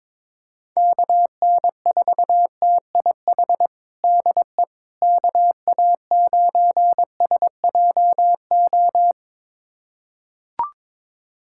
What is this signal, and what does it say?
Morse code, heard as a keyed tone, about 22 wpm, 700 Hz. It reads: KN4TIH DE KA9SJO
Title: USA Callsign DE Callsign - 6 Characters - 22wpm